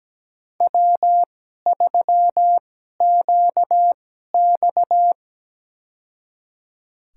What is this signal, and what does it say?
Morse code, heard as a keyed tone, about 17 wpm, spaced as for 12 wpm, 700 Hz. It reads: W3QX
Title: USA Call Signs - 4 Characters - 2x Character Spacing - 17wpm